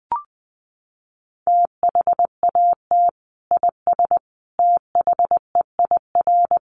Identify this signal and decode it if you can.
Morse code, heard as a keyed tone, about 20 wpm, 700 Hz. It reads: THAT IS THEIR
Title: Sentences from Top 1300 Words 20wpm